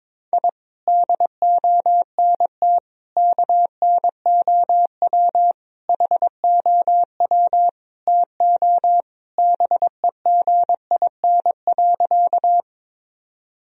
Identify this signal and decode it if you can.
Morse code, heard as a keyed tone, about 22 wpm, 700 Hz. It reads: I DONT KNOW HOW TO BEGIN.
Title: Sentences from Top 1000 Words - Review Part 1 - 22wpm